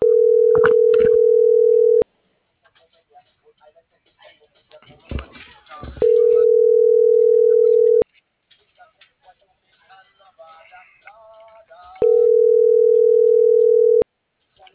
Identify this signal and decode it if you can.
Ringback tone (North America), 440 + 480 Hz.